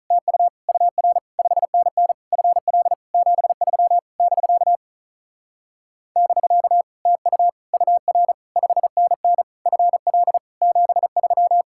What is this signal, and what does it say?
Morse code, heard as a keyed tone, about 46 wpm, 700 Hz. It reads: TU UR 5NN FL 73 <BK> <BK> TU UR 5NN FL 73